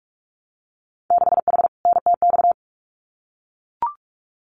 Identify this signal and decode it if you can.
Morse code, heard as a keyed tone, about 44 wpm, 690 Hz. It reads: 6H NTX
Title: Winter Field Day Exchanges Encore 45wpm